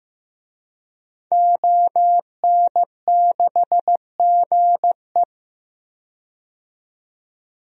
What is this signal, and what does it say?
Morse code, heard as a keyed tone, about 15 wpm, 700 Hz. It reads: ON6GE
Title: DXCC-100 Call Signs - 5 Characters - 15wpm